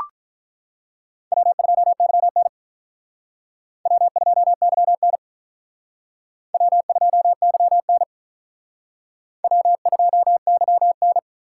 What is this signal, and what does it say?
Morse code, heard as a keyed tone, about 44 wpm, 700 Hz. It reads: W2YN W2YN W2YN W2YN